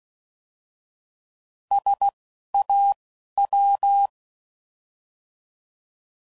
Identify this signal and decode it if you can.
Morse code, heard as a keyed tone, about 16 wpm, spaced as for 12 wpm, 800 Hz. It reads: SAW
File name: Prefixes / By First Letter / Ss